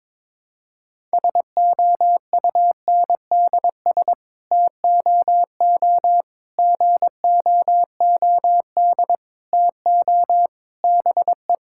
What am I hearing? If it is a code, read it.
Morse code, heard as a keyed tone, about 22 wpm, 700 Hz. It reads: SOUNDS TOO GOOD TO BE
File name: Sets of 9 Words - 22wpm